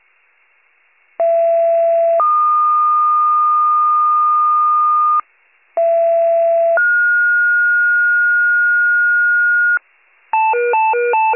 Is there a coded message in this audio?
Yes: two-tone page from 1.2 to 9.8 s, heard as a one-second tone, then a three-second tone, about 670 then 1220 Hz, about 670 then 1430 Hz, with noise.